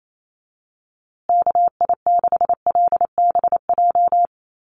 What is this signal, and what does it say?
Morse code, heard as a keyed tone, about 28 wpm, 700 Hz. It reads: KI6LBJ
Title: USA Call Signs Repeated 2x 28wpm